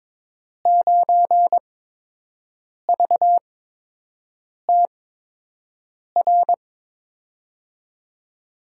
Morse code, heard as a keyed tone, about 22 wpm, spaced as for 6 wpm, 700 Hz. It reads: 9VTR
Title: USA Call Signs - 5 Characters - 8x Character Spacing - 22wpm